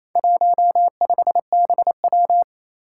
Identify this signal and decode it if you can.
Morse code, heard as a keyed tone, about 28 wpm, 700 Hz. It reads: 15BW
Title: Four Challenging Letter-Number Combinations 28wpm